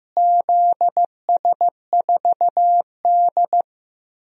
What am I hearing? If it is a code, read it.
Morse code, heard as a keyed tone, about 15 wpm, 700 Hz. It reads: ZS4D
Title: DXCC-100 Call Signs - 4 Characters - Repeated 3x - 15wpm